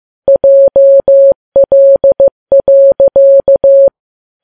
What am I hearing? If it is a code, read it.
Morse code, heard as a keyed tone, about 15 wpm, 550 Hz. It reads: JL.
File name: Files Web tls / Morse on9fv / morse15kps